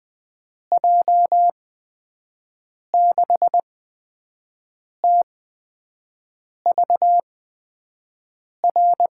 Morse code, heard as a keyed tone, about 20 wpm, spaced as for 5 wpm, 700 Hz. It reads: J6TVR
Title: USA Call Signs - 6 Characters - 8x Character Spacing - 20wpm